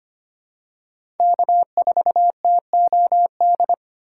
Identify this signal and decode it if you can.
Morse code, heard as a keyed tone, about 25 wpm, 700 Hz. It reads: K4TOD
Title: USA Call Signs - 5 Characters - 3x Character Spacing - 25wpm